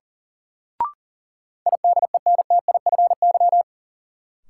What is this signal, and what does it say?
Morse code, heard as a keyed tone, about 40 wpm, 700 Hz. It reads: IDENTIFY
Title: Sentences from Top 900 Words - New Word Drill - 40wpm